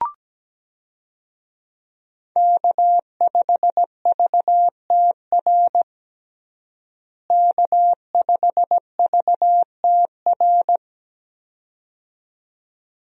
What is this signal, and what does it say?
Morse code, heard as a keyed tone, about 17 wpm, 700 Hz. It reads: K5VTR K5VTR
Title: USA Call Signs - 5 Characters - Repeated 2x - 17wpm